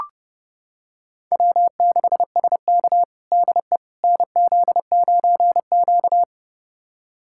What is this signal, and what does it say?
Morse code, heard as a keyed tone, about 32 wpm, 700 Hz. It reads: W6SK DE NZ9Q
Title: USA Callsign DE Callsign - 4 Characters - 30wpm